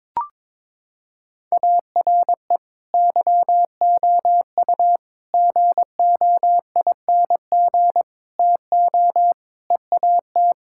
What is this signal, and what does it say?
Morse code, heard as a keyed tone, about 22 wpm, 700 Hz. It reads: ARE YOU GOING TO EAT
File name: practice / 22wpm / Sentences with 4-Word Phrases - 22wpm